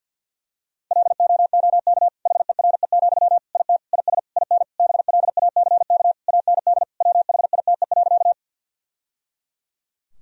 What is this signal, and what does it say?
Morse code, heard as a keyed tone, about 47 wpm, 700 Hz. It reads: LOOK HERE, IT IS IN BLACK AND WHITE.
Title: Sentences from Top 1000 Words - Review Part 2 - 50wpm